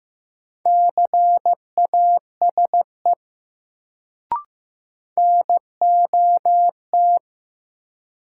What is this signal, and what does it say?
Morse code, heard as a keyed tone, about 15 wpm, 700 Hz. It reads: CASE NOT
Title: Sets of 2 Words - New Word Drill - 15wpm